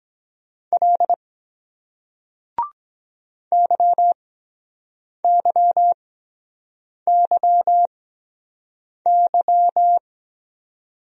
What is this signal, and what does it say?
Morse code, heard as a keyed tone, about 23 wpm, spaced as for 7 wpm, 700 Hz. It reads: L YYYY